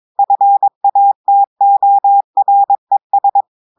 Morse code, heard as a keyed tone, about 22 wpm, 800 Hz. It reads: FATORES
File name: audios / portugais / portugais texte en clair 2 vitesse 22 mots minutes